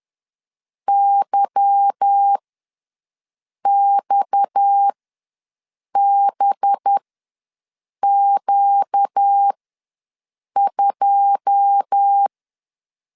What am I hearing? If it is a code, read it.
Morse code, heard as a keyed tone, about 10 wpm, spaced as for 6 wpm, 800 Hz. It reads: YXBQ2
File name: LEZIONI / 10 SIGNS / 10A SLASH ws 10wpm 4121